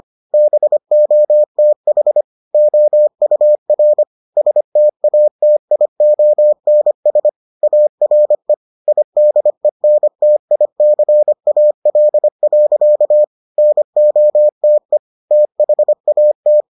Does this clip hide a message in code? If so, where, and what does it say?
Yes: Morse code from 0.3 to 16.6 s, heard as a keyed tone, about 25 wpm, 600 Hz. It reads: BOTH OUR STATIONS ARE IDENTICAL. NOTE THAT